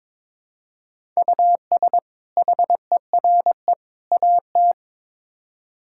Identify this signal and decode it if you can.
Morse code, heard as a keyed tone, about 22 wpm, 700 Hz. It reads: US HERE AT
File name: practice / 22wpm / Build A Phrase - 5-Word Phrases - 22wpm